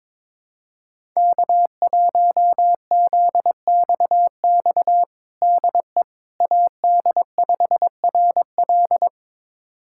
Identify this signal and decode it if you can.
Morse code, heard as a keyed tone, about 22 wpm, 700 Hz. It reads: K1ZXX DE AD5RL
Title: USA Callsign DE Callsign - 5 Characters - 22wpm